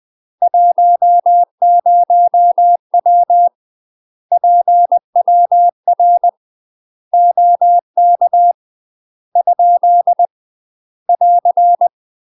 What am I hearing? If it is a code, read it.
Morse code, heard as a keyed tone, about 20 wpm, 700 Hz. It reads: 10W PWR OK ? <AR>
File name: UK MASTER QSOs TRAINING / 20WPM / UK MASTER QSOs 0011